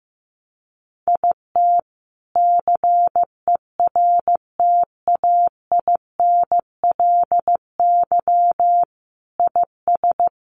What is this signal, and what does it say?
Morse code, heard as a keyed tone, about 15 wpm, 700 Hz. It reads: IT CERTAINLY IS
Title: Sentences from Top 1200 Words 15wpm